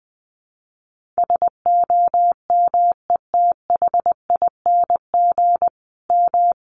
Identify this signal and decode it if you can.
Morse code, heard as a keyed tone, about 20 wpm, 700 Hz. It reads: SOMETHING M
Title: Sentences from Top 1400 Words 20wpm